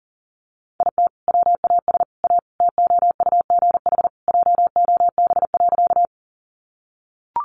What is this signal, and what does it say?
Morse code, heard as a keyed tone, about 40 wpm, 700 Hz. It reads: IT WAS A TOUGH JOB.
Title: Sentences from Top 1200 Words 40wpm